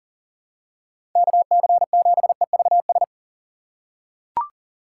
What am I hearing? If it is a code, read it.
Morse code, heard as a keyed tone, about 40 wpm, 700 Hz. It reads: KC7EVS
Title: USA Call Signs - 6 Characters - 6x Character Spacing - 40wpm